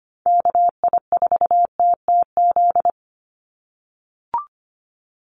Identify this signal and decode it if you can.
Morse code, heard as a keyed tone, about 25 wpm, 700 Hz. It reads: KI4TTZ
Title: USA Call Signs 25wpm